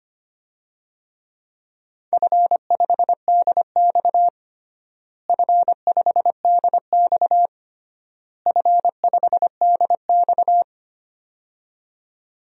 Morse code, heard as a keyed tone, about 25 wpm, 700 Hz. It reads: F5DX F5DX F5DX
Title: DXCC-100 Call Signs - 4 Characters - Repeated 3x - 25wpm